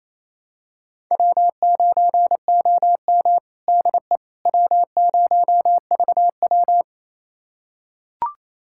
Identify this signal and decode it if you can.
Morse code, heard as a keyed tone, about 28 wpm, 700 Hz. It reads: W9OM DE W0VW
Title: USA Callsign DE Callsign - 4 Characters - 28wpm